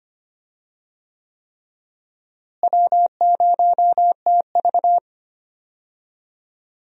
Morse code, heard as a keyed tone, about 25 wpm, 700 Hz. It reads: W0TV
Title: USA Call Signs - 4 Characters - Repeated 3x - 25wpm